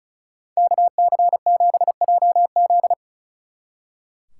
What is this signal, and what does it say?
Morse code, heard as a keyed tone, about 45 wpm, 700 Hz. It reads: KC7JZ